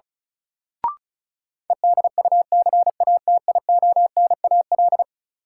Morse code, heard as a keyed tone, about 34 wpm, 700 Hz. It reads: EDUCATIONAL
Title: Sentences from Top 1200 Words - New Word Drill - 35wpm